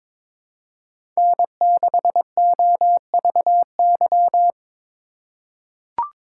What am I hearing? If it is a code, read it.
Morse code, heard as a keyed tone, about 22 wpm, 700 Hz. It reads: N6OVY
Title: USA Call Signs - 5 Characters - 2x Character Spacing - 22wpm